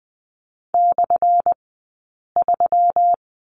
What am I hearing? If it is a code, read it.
Morse code, heard as a keyed tone, about 20 wpm, spaced as for 8 wpm, 700 Hz. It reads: /3
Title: 177 Lesson 29d - Slash - Two Characters 20wpm